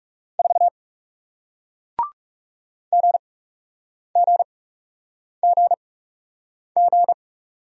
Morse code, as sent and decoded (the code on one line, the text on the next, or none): ....- / --. --. --. --.
4 GGGG